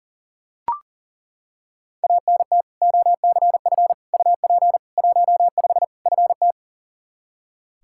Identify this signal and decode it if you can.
Morse code, heard as a keyed tone, about 40 wpm, 700 Hz. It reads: ANT OCF UP 15 FT
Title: Practice QSO - Exchange 2 - Info - 40wpm